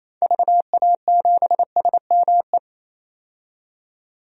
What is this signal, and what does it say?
Morse code, heard as a keyed tone, about 28 wpm, 700 Hz. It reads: VA7SME
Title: DXCC-100 Call Signs - 6 Characters - 28wpm